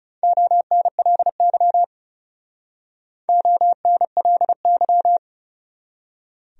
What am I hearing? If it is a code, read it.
Morse code, heard as a keyed tone, about 37 wpm, 700 Hz. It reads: ONLY ONLY
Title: Speed-Racing Top 500 Words 30wpm